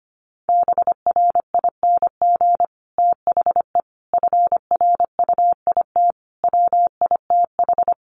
Morse code, heard as a keyed tone, about 25 wpm, 700 Hz. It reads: BRING THE FRUIT WITH
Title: Sentences from Top 1300 Words 25wpm